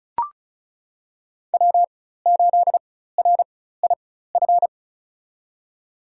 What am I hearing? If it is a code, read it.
Morse code, heard as a keyed tone, about 34 wpm, spaced as for 16 wpm, 700 Hz. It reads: W8RIF
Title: USA Call Signs - 4x Character Spacing - 35wpm